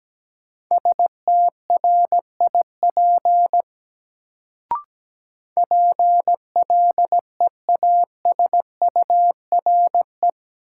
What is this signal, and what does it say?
Morse code, heard as a keyed tone, about 17 wpm, 700 Hz. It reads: STRIP PLEASURE
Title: Sentences from Top 1600 Words - New Word Drill Alt - 17wpm